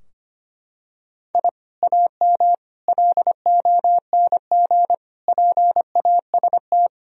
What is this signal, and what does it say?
Morse code, heard as a keyed tone, about 25 wpm, 700 Hz. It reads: I AM LONG PAST
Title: Sentences from Top 1000 Words - Review Part 3 - 25wpm